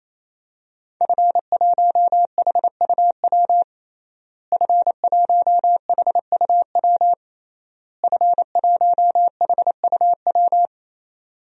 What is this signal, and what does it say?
Morse code, heard as a keyed tone, about 28 wpm, 700 Hz. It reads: F1HUW F1HUW F1HUW
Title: DXCC-100 Call Signs - 5 Characters - Repeated 3x - 28wpm